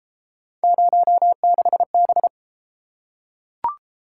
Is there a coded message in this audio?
Yes: Morse code from 0.6 to 2.3 s, heard as a keyed tone, about 32 wpm, 700 Hz. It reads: 06B